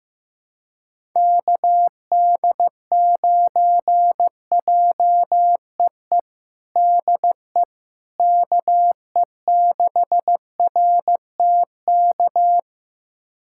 Morse code, heard as a keyed tone, about 15 wpm, 700 Hz. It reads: KD9JEE DE KE6RTK
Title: USA Callsign DE Callsign - 6 Characters - 15wpm